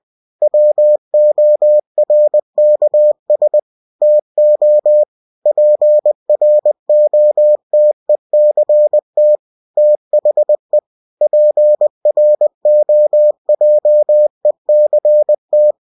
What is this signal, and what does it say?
Morse code, heard as a keyed tone, about 20 wpm, 600 Hz. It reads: WORKS TO PROTECT THE PROJECT
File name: cw-books / winnie-the-pooh / winnie-20wpm-page-0112